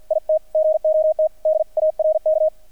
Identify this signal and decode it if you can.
Morse code, heard as a keyed tone, about 48 wpm, 620 Hz, with radio noise. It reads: IT GOT DARK
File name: archived / CRSnet